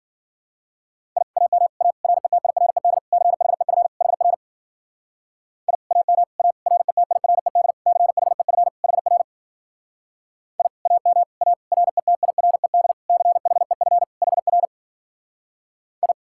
Morse code, heard as a keyed tone, about 52 wpm, 700 Hz. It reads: I AM A RETIRED CHEF HR I AM A RETIRED CHEF HR I AM A RETIRED CHEF HR I